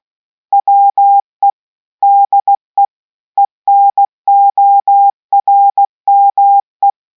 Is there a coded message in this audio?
Yes: Morse code from 0.5 to 6.9 s, heard as a keyed tone, about 16 wpm, 800 Hz. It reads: WE DE ENORME